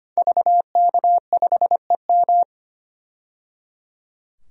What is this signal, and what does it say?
Morse code, heard as a keyed tone, about 25 wpm, 700 Hz. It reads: VK5EM
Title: DXCC-100 Call Signs - 5 Characters - 25wpm